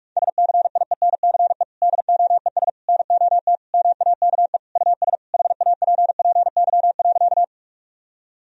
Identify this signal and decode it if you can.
Morse code, heard as a keyed tone, about 41 wpm, 700 Hz. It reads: SCIENCE DOES NOT MAKE US HAPPY.
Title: Sentences from Top 1000 Words - Review Part 2 - 45wpm